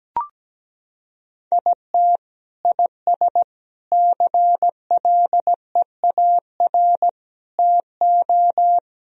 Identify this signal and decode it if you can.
Morse code, heard as a keyed tone, about 17 wpm, 700 Hz. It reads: IT IS CLEAR TO
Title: Sentences from Top 1000 Words - Review Part 2 - 17wpm